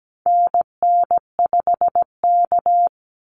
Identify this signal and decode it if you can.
Morse code, heard as a keyed tone, about 17 wpm, 700 Hz. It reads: NN5K
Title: NA SOTA Activator Callsigns 17wpm